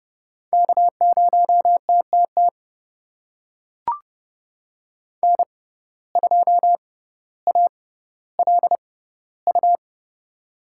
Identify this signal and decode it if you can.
Morse code, heard as a keyed tone, about 32 wpm, spaced as for 16 wpm, 700 Hz. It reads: K0TTT N2ALU
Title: USA Call Signs - 5 Characters - 6x Character Spacing - 30wpm